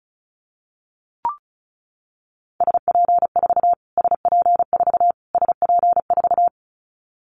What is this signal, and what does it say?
Morse code, heard as a keyed tone, about 34 wpm, 700 Hz. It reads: SP4 SP4 SP4
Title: Callsign Prefixes for Top 25 DXCC - Repeated 3x - 35wpm